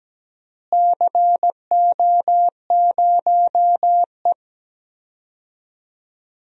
Morse code, heard as a keyed tone, about 17 wpm, 700 Hz. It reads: CO0E
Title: DXCC-100 Call Signs - 4 Characters - Repeated 2x - 17wpm